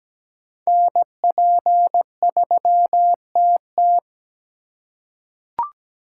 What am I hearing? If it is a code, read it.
Morse code, heard as a keyed tone, about 17 wpm, 700 Hz. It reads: NP3TT
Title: USA Call Signs - 5 Characters - 8x Character Spacing - 17wpm